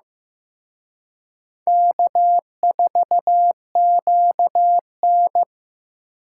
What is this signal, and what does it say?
Morse code, heard as a keyed tone, about 18 wpm, 700 Hz. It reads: K4QN
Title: Speed Racing - USA Call Signs 15wpm